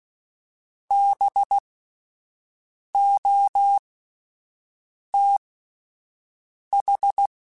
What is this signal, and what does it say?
Morse code, heard as a keyed tone, about 16 wpm, spaced as for 5 wpm, 800 Hz. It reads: BOTH
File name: Spaced Alphas / Spaced Alphas mp3s / Bs sp @10